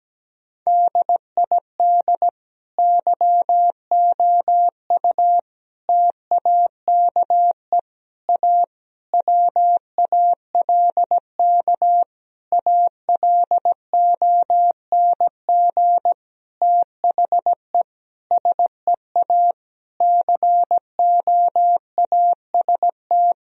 Morse code, heard as a keyed tone, about 17 wpm, 700 Hz. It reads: DID YOU TAKE A WALK ALONG THE SEA COAST
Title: Sentences from Top 1000 Words - Review Part 4 - 17wpm